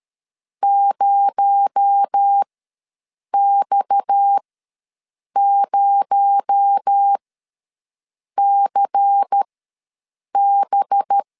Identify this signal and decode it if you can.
Morse code, heard as a keyed tone, about 13 wpm, spaced as for 6 wpm, 800 Hz. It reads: 0X0CB
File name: LEZIONI / 6 PKX0 / 6 PKX0 ws 12wpm 3448